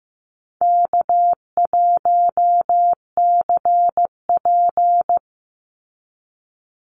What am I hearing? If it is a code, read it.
Morse code, heard as a keyed tone, about 15 wpm, 700 Hz. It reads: K1CP
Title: Call Signs Repeated 2x - 15wpm